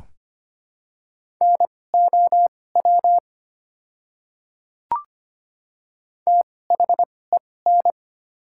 Morse code, heard as a keyed tone, about 25 wpm, spaced as for 18 wpm, 690 Hz. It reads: NOW THEN
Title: 2x Character Spacing - Top 200 Words - 25wpm